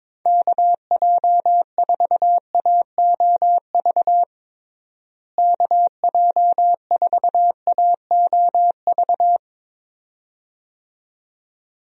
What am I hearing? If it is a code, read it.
Morse code, heard as a keyed tone, about 22 wpm, 700 Hz. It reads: KJ4AOV KJ4AOV
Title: USA Call Signs - 6 Characters - Repeated 2x - 22wpm